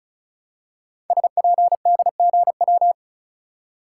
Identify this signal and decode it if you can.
Morse code, heard as a keyed tone, about 36 wpm, 700 Hz. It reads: SPDGW
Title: Code Groups - 5 Letters - 35wpm